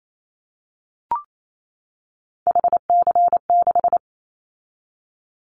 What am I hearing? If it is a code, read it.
Morse code, heard as a keyed tone, about 28 wpm, 700 Hz. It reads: HC6
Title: Callsign Prefixes for Top 100 DXCC - 28wpm